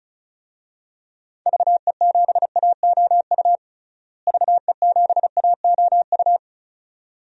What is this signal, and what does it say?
Morse code, heard as a keyed tone, about 34 wpm, 700 Hz. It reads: VE7AOU VE7AOU
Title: DXCC-100 Call Signs - 6 Characters - Repeated 2x - 35wpm